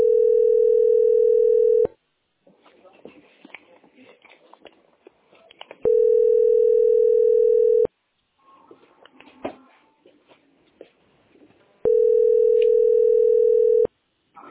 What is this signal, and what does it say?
Ringback tone (North America), 440 + 480 Hz.